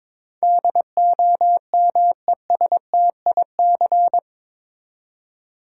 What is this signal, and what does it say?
Morse code, heard as a keyed tone, about 22 wpm, 700 Hz. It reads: DOMESTIC
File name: practice / 22wpm / Eight Letter Words - 22wpm